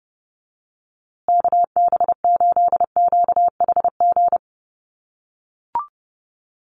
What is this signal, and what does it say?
Morse code, heard as a keyed tone, about 30 wpm, 700 Hz. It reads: KB8QHG
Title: USA Call Signs Repeated 2x 30wpm